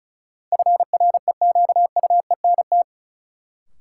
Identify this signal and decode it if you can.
Morse code, heard as a keyed tone, about 36 wpm, 700 Hz. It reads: FREQUENT
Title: Sentences from Top 1400 Words - New Word Drill Alt - 35wpm